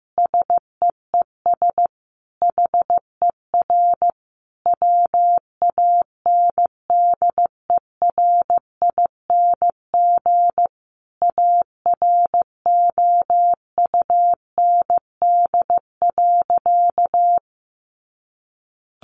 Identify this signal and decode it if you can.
Morse code, heard as a keyed tone, about 15 wpm, 700 Hz. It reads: SEES HER WANDERING AROUND.
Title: Additional Easy Phrases 15wpm